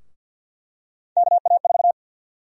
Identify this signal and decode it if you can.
Morse code, heard as a keyed tone, about 51 wpm, 700 Hz. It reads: KA4
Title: USA Call Sign Prefixes - Repeated 2x - 50wpm